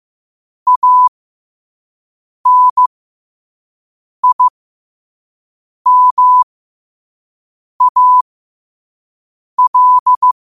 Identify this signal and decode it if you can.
Morse code, heard as a keyed tone, about 15 wpm, spaced as for 5 wpm, 1000 Hz. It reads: ANIMAL
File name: qotdfiles / 5wpm